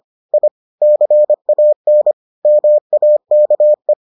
Morse code, heard as a keyed tone, about 25 wpm, 600 Hz. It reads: I CAN MAKE